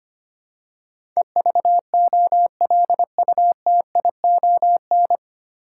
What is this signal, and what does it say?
Morse code, heard as a keyed tone, about 25 wpm, 700 Hz. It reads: EVOLUTION
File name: practice / 25wpm / 9 Letter Words - First 3 Letters Given - 25wpm